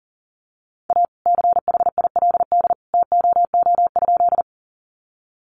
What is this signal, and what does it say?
Morse code, heard as a keyed tone, about 40 wpm, 690 Hz. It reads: A CHILD TOO?
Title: Sentences from Top 800 Words 40wpm